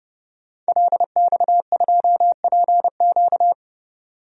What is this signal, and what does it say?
Morse code, heard as a keyed tone, about 30 wpm, 700 Hz. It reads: LX2PQ
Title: DXCC-100 Call Signs - 5 Characters - 30wpm